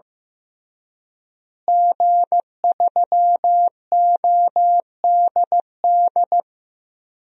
Morse code, heard as a keyed tone, about 15 wpm, 700 Hz. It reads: G3ODD
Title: DXCC-100 Call Signs - 5 Characters - Repeated 3x - 15wpm